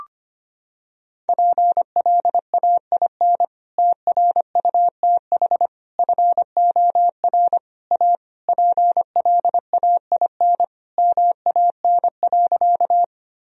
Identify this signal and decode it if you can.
Morse code, heard as a keyed tone, about 25 wpm, 700 Hz. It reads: PLAIN TRUTH FOR A PLAIN MAN.
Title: Sentences from Top 1000 Words - Review Part 2 - 25wpm